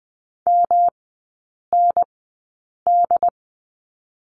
Morse code, heard as a keyed tone, about 20 wpm, spaced as for 8 wpm, 700 Hz. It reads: MND
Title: 166 Lesson 27e - Number 7 - Three Characters 20wpm